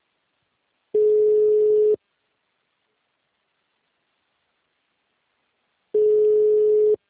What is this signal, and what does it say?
Ringback tone (Europe, 425 Hz).
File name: out- -2022- - - .1799